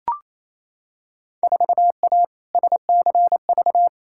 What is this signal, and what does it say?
Morse code, heard as a keyed tone, about 28 wpm, 700 Hz. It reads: 4A SCV
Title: Field Day Exchanges Encore - 28wpm